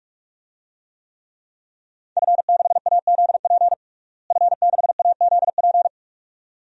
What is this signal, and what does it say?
Morse code, heard as a keyed tone, about 44 wpm, 700 Hz. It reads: F6AZP F6AZP
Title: DXCC-100 Call Signs - 5 Characters - Repeated 2x - 45wpm